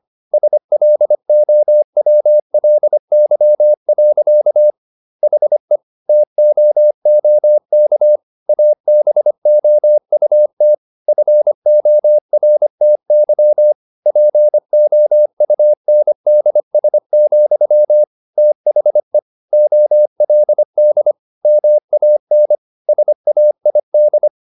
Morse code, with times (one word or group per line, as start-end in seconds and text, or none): 0.3-4.7 SLOWLY.
5.2-5.8 HE
6.1-8.2 TOOK
8.5-10.7 ABOUT
11.1-13.7 FORTY
14.1-18.0 POUNDS,
18.4-19.2 THE
19.5-21.1 OLD
21.4-22.6 MAN
22.9-24.3 SAID